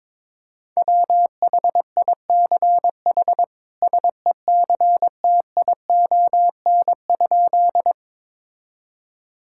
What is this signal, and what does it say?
Morse code, heard as a keyed tone, about 21 wpm, 700 Hz. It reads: WHICH SECTION?
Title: Sentences from Top 1000 Words - Review Part 5 - 22wpm